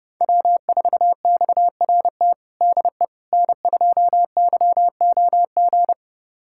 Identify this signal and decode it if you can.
Morse code, heard as a keyed tone, about 28 wpm, 700 Hz. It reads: W4XRT DE N2YOG
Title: USA Callsign DE Callsign 30wpm